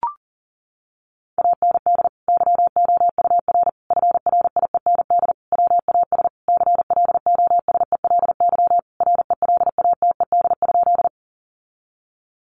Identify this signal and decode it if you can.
Morse code, heard as a keyed tone, about 37 wpm, 700 Hz. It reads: AND YOUR FRIEND WAS CLOSELY RELATED?
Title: Sentences from Top 1400 Words 40wpm